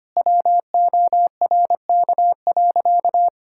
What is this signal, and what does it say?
Morse code, heard as a keyed tone, about 25 wpm, 700 Hz. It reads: WORK.
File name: practice / 25wpm / Sentences with 3-Word Phrases - 25wpm